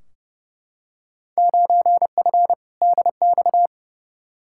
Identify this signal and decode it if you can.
Morse code, heard as a keyed tone, about 30 wpm, 700 Hz. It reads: 9F DX
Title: Field Day Exchanges Encore - 30wpm